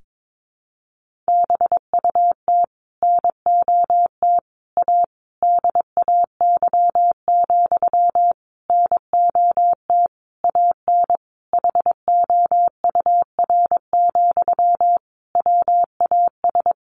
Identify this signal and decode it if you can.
Morse code, heard as a keyed tone, about 22 wpm, 700 Hz. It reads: BUT NOT A DAY, NOT AN HOUR, WAS
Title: Sentences from Top 1400 Words 22wpm